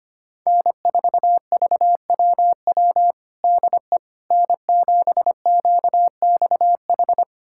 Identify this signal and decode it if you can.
Morse code, heard as a keyed tone, about 25 wpm, 700 Hz. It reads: N4VWW DE N7QXH
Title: USA Callsign DE Callsign - 5 Characters - 25wpm